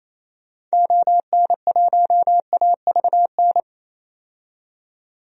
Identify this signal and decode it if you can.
Morse code, heard as a keyed tone, about 27 wpm, 700 Hz. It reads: ON1AVN
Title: DXCC-100 Call Signs - 6 Characters - 28wpm